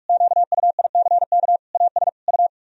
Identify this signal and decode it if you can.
Morse code, heard as a keyed tone, about 46 wpm, 700 Hz. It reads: QUICK AS U
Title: Sentences from Top 500 Words and Abbreviations 45wpm